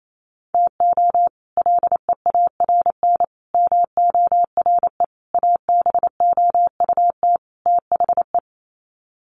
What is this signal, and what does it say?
Morse code, heard as a keyed tone, about 28 wpm, 700 Hz. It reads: TO LEARN MORE ABOUT THE
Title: Sets of 5 Words Encore2 28wpm